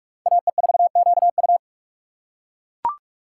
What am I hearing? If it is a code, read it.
Morse code, heard as a keyed tone, about 39 wpm, 700 Hz. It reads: AE4QU